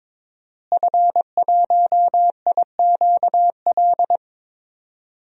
Morse code, heard as a keyed tone, about 22 wpm, 700 Hz. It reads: F1IQL
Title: DXCC-100 Call Signs - 5 Characters - Repeated 2x - 22wpm